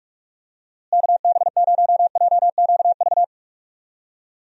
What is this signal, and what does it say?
Morse code, heard as a keyed tone, about 48 wpm, 700 Hz. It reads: KB0JQV
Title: USA Call Signs - 6 Characters - Repeated 3x - 45wpm